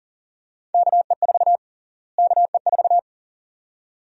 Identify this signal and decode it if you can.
Morse code, heard as a keyed tone, about 40 wpm, 700 Hz. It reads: KE4 KE4
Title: USA Call Sign Prefixes - Repeated 2x - 40wpm